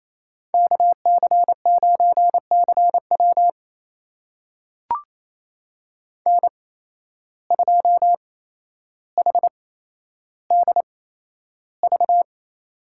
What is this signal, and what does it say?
Morse code, heard as a keyed tone, about 28 wpm, spaced as for 12 wpm, 700 Hz. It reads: KC9CW N2HDV
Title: USA Call Signs - 5 Characters - 8x Character Spacing - 28wpm